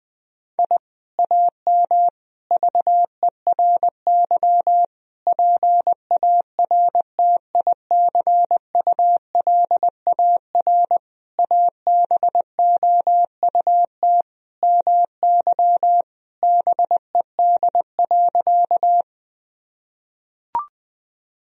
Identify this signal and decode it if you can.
Morse code, heard as a keyed tone, about 20 wpm, 700 Hz. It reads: I AM VERY PARTICULAR ABOUT MY BED.
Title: Sentences from Top 1000 Words - Review Part 4 - 20wpm